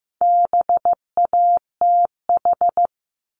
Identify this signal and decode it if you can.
Morse code, heard as a keyed tone, about 15 wpm, 700 Hz. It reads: BATH
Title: Top 1501-1750 Words 15wpm